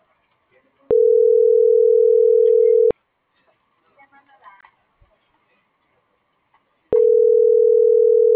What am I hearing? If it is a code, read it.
Ringback tone (North America), 440 + 480 Hz.